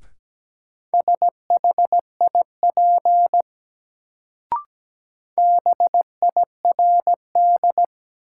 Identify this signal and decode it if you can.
Morse code, heard as a keyed tone, about 17 wpm, 700 Hz. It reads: SHIP BIRD
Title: Sentences from Top 400 Words - New Word Drill - 17wpm